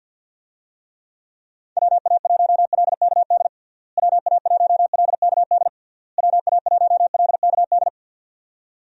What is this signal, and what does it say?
Morse code, heard as a keyed tone, about 52 wpm, 700 Hz. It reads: WA1LKD WA1LKD WA1LKD
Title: USA Call Signs - 6 Characters - Repeated 3x - 50wpm